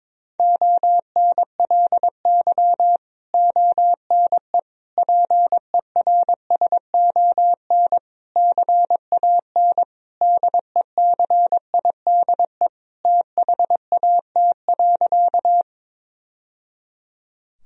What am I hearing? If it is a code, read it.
Morse code, heard as a keyed tone, about 22 wpm, 700 Hz. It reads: ONLY ONE PERSON CAN DECIDE THAT.
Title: Sentences from Top 1000 Words - Review Part 3 - 22wpm